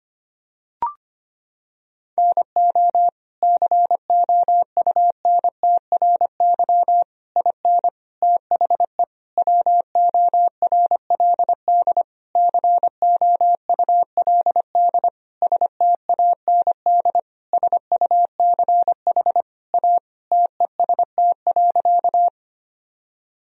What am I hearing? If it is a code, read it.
Morse code, heard as a keyed tone, about 25 wpm, 700 Hz. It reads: NO COUNTRY IN THE WORLD COULD STAND SUCH A TEST.
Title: Sentences from Top 1000 Words - Review Part 3 - 25wpm